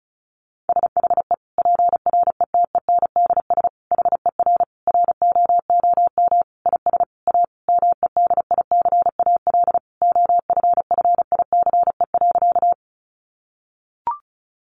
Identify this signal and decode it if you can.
Morse code, heard as a keyed tone, about 34 wpm, 700 Hz. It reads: SHE PRETENDS HER ROOM IS A MEDICAL OFFICE.
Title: Additional Easy Phrases 35wpm